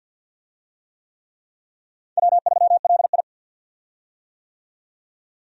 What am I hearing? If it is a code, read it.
Morse code, heard as a keyed tone, about 50 wpm, 700 Hz. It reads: W3LI
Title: USA Call Signs - 4 Characters - 50wpm